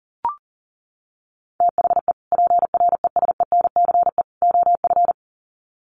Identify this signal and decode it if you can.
Morse code, heard as a keyed tone, about 40 wpm, 700 Hz. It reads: THE PRESENCE OF
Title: Sets of 3 Words 40wpm